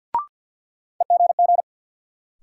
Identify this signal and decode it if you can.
Morse code, heard as a keyed tone, about 52 wpm, 700 Hz. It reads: EGG
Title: Sentences from Top 1000 Words - Review Part 7 - New Word Drill - 50wpm